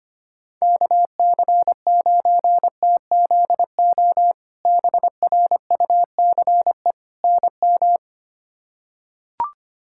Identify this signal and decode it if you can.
Morse code, heard as a keyed tone, about 25 wpm, 700 Hz. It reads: KC9TZO BRUCE NM
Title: K1USN SST QSO Practice - 25wpm